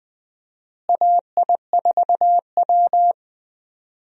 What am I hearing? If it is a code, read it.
Morse code, heard as a keyed tone, about 20 wpm, 700 Hz. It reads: AI4W
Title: USA Call Signs - 4 Characters - Repeated 2x - 20wpm